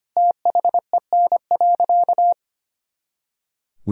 Morse code, heard as a keyed tone, about 25 wpm, 700 Hz. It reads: THEN.
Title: Sentences from Top 200 Words and Abbreviations 25wpm